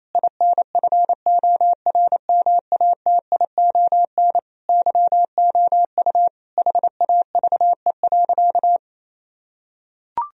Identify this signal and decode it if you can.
Morse code, heard as a keyed tone, about 28 wpm, 700 Hz. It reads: INFORMATION YOU HAVE.
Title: Sentences from Top 1000 Words - Review Part 3 - 28wpm